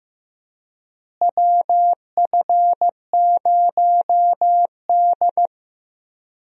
Morse code, heard as a keyed tone, about 15 wpm, 700 Hz. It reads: WF0D
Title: USA Call Signs - 4 Characters - Repeated 3x - 15wpm